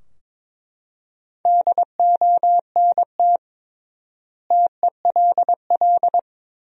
Morse code, heard as a keyed tone, about 22 wpm, 700 Hz. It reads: DONT TELL
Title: 3x Word-Spacing - Sentences from Top 300 Words - 22wpm